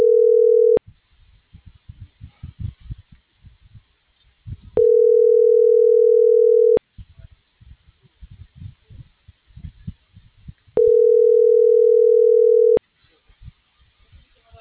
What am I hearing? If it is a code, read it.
Ringback tone (North America), 440 + 480 Hz.
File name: CENTRAL MAERSK / out adolfo.msk - .28125